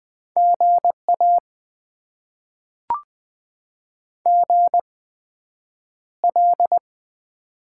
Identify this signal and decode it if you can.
Morse code, heard as a keyed tone, about 20 wpm, spaced as for 9 wpm, 700 Hz. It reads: GA GL
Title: Top 90 QSO Elements - 8x Character Spacing - 20wpm